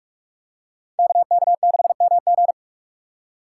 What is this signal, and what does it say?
Morse code, heard as a keyed tone, about 47 wpm, 700 Hz. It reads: KK6MG
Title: USA Call Signs - 5 Characters - 6x Character Spacing - 45wpm